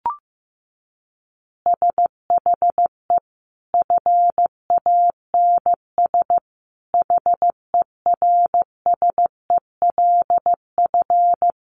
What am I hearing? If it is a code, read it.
Morse code, heard as a keyed tone, about 15 wpm, 700 Hz. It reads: SHE FANS HERSELF
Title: Easy Phrases 15wpm